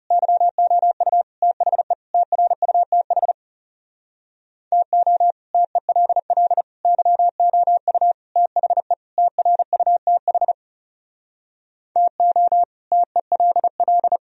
Morse code, heard as a keyed tone, about 40 wpm, 700 Hz. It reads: YOU THE TRUTH TO TELL YOU THE TRUTH TO TELL